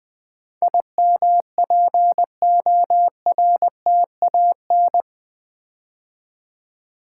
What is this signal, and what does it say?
Morse code, heard as a keyed tone, about 20 wpm, 700 Hz. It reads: IMPORTAN
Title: Build A Word - 4 Letters At A Time - 10-Letter Words - 20wpm